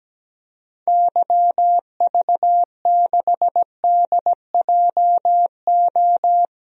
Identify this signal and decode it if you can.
Morse code, heard as a keyed tone, about 17 wpm, 700 Hz. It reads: YV6DJO
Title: DXCC-100 Call Signs - 6 Characters - 17wpm